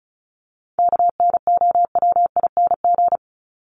Morse code, heard as a keyed tone, about 34 wpm, 700 Hz. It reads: KNOWING
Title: ING Suffix - Top 100 Words with Base Words 35wpm